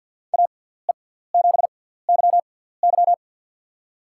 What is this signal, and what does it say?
Morse code, heard as a keyed tone, about 47 wpm, 700 Hz. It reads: A E 7 Y Y
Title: USA Call Signs - 6x Character Spacing - 50wpm